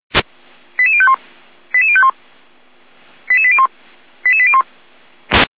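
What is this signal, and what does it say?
Five-tone selective call (ZVEI), heard as quick runs of five tones, tones of about 70 ms. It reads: 88051 88051 88801 88801
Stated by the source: Five-tone ZVEI selective call received by SDR.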